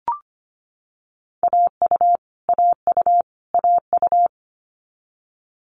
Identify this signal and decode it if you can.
Morse code, heard as a keyed tone, about 25 wpm, 700 Hz. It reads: AU AU AU
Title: Multi-letter Phonemes Repeated 3x 25wpm